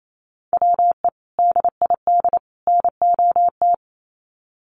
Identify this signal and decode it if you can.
Morse code, heard as a keyed tone, about 28 wpm, 700 Hz. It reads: WE DID NOT
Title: Sets of 3 Words 28wpm